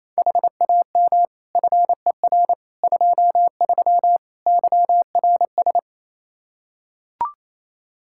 Morse code, heard as a keyed tone, about 28 wpm, 700 Hz. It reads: HAM FER 23 YRS
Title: Practice QSO - Exchange 3 - More Info - 28wpm